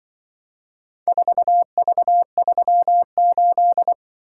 Morse code, heard as a keyed tone, about 24 wpm, 700 Hz. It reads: 4V38
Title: Speed-Racing - Four Letter-Number Combinations - 20wpm